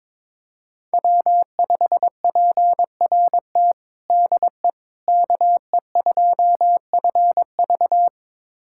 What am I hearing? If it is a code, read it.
Morse code, heard as a keyed tone, about 22 wpm, 700 Hz. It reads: W5PRT DE KE2FV
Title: USA Callsign DE Callsign - 5 Characters - 22wpm